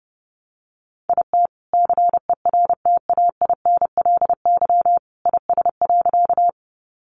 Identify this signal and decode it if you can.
Morse code, heard as a keyed tone, about 28 wpm, 700 Hz. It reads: IT CERTAINLY IS.
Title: Sentences from Top 900 Words 30wpm